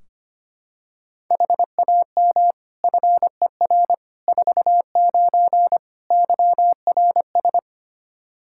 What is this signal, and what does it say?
Morse code, heard as a keyed tone, about 25 wpm, 700 Hz. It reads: HAM FER 49 YRS
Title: Practice QSO - Exchange 3 - More Info - 25wpm